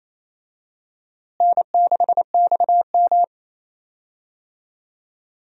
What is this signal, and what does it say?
Morse code, heard as a keyed tone, about 27 wpm, 700 Hz. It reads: N6XM
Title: USA Call Signs - 4 Characters - Repeated 2x - 28wpm